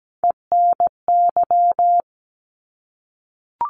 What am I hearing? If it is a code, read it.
Morse code, heard as a keyed tone, about 17 wpm, 700 Hz. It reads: ENY
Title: ARRL Sections 17wpm